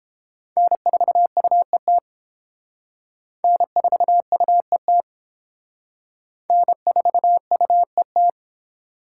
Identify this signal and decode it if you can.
Morse code, heard as a keyed tone, about 28 wpm, 700 Hz. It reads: N4UET N4UET N4UET